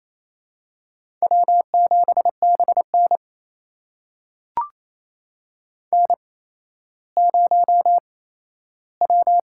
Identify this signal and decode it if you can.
Morse code, heard as a keyed tone, about 28 wpm, 700 Hz. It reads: W 7 B N N 0 W
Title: USA Call Signs - 4 Characters - 8x Character Spacing - 28wpm